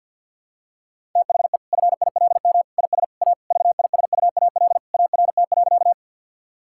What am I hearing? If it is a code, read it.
Morse code, heard as a keyed tone, about 46 wpm, 700 Hz. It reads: THE FILM IS A VISUAL ART.
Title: Sentences from Top 1000 Words - Review Part 6 - 50wpm